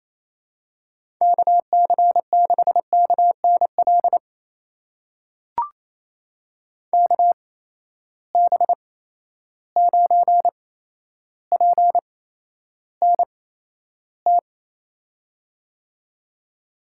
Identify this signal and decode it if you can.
Morse code, heard as a keyed tone, about 28 wpm, spaced as for 12 wpm, 700 Hz. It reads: KC6KNL KB9PNT
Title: USA Call Signs - 8x Character Spacing - 28wpm